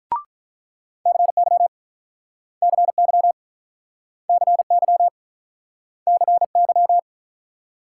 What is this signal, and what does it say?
Morse code, heard as a keyed tone, about 47 wpm, 700 Hz. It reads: CY CY CY CY